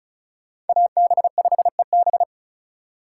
Morse code, heard as a keyed tone, about 46 wpm, 700 Hz. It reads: AB5EB